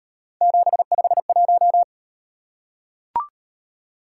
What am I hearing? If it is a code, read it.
Morse code, heard as a keyed tone, about 37 wpm, 700 Hz. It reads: 751